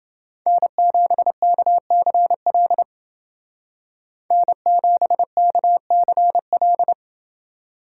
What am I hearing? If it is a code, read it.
Morse code, heard as a keyed tone, about 29 wpm, 700 Hz. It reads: N7KCL N7KCL